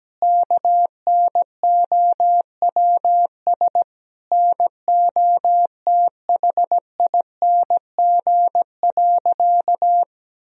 Morse code, heard as a keyed tone, about 17 wpm, 700 Hz. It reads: KNOWS NOTHING.
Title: Sentences from Top 1000 Words - Review Part 1 - 17wpm